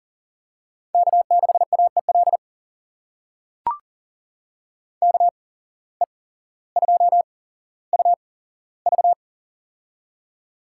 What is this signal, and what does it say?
Morse code, heard as a keyed tone, about 40 wpm, spaced as for 17 wpm, 700 Hz. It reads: K6AEL KE2UV
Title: USA Call Signs - 5 Characters - 8x Character Spacing - 40wpm